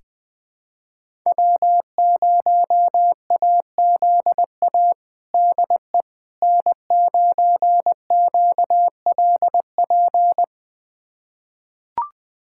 Morse code, heard as a keyed tone, about 21 wpm, 700 Hz. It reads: W0AZA DE N9QLP
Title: USA Callsign DE Callsign - 5 Characters - 20wpm